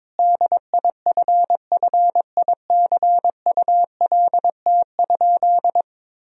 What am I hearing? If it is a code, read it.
Morse code, heard as a keyed tone, about 22 wpm, 700 Hz. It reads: DIFFICULT?
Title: Sentences from Top 1000 Words - Review Part 4 - 22wpm